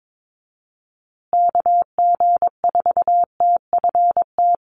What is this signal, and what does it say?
Morse code, heard as a keyed tone, about 22 wpm, 700 Hz. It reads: KG4TFT
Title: USA Call Signs Repeated 2x 22wpm